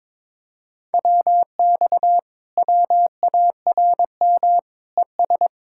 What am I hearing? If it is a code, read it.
Morse code, heard as a keyed tone, about 22 wpm, 700 Hz. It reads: WX WARM ES
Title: Speed-Racing Common QSO Phrases Encore - 22wpm